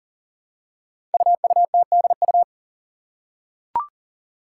Morse code, heard as a keyed tone, about 43 wpm, 700 Hz. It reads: UUTDU
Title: Code Groups - 5 Letters - 40wpm